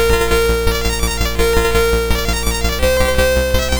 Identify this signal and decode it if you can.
There is no code.